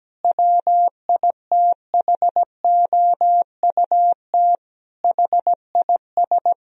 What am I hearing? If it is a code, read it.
Morse code, heard as a keyed tone, about 17 wpm, 700 Hz. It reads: WITHOUT HIS
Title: Sentences from Top 1000 Words - Review Part 3 - 17wpm